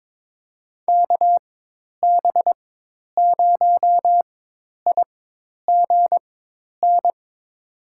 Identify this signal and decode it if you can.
Morse code, heard as a keyed tone, about 22 wpm, spaced as for 10 wpm, 700 Hz. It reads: KB0IGN
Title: USA Call Signs - 6 Characters - 4x Character Spacing - 22wpm